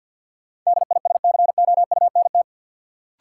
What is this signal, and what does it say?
Morse code, heard as a keyed tone, about 53 wpm, 700 Hz. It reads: DISCOUNT
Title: Sentences from Top 1200 Words - New Word Drill - 50wpm